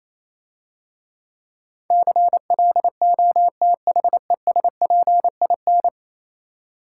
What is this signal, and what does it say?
Morse code, heard as a keyed tone, about 28 wpm, 700 Hz. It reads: CLOTHESPIN